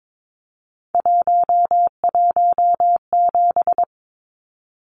Morse code, heard as a keyed tone, about 22 wpm, 700 Hz. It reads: 117
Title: Numbers - 3-Digit 22wpm